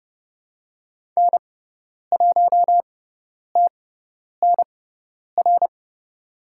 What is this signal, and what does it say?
Morse code, heard as a keyed tone, about 30 wpm, spaced as for 10 wpm, 700 Hz. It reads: N1TNR
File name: Lesson 16g - Letter c - Call Signs 30wpm